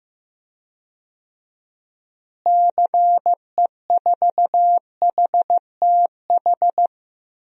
Morse code, heard as a keyed tone, about 15 wpm, 700 Hz. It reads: CE4HTH
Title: DXCC-100 Call Signs - 6 Characters - Repeated 2x - 15wpm